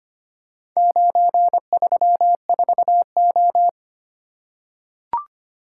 Morse code, heard as a keyed tone, about 25 wpm, 700 Hz. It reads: 934O
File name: practice / 25wpm / Four Letter-Number Combinations - 25wpm